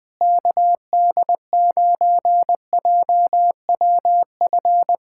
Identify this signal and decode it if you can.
Morse code, heard as a keyed tone, about 20 wpm, 700 Hz. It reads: KD9JWF
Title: USA Callsign DE Callsign - 6 Characters - 20wpm